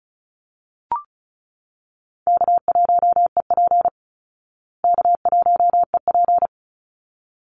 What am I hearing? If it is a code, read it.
Morse code, heard as a keyed tone, about 34 wpm, 700 Hz. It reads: K1EP K1EP
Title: Call Signs Repeated 3x - 35wpm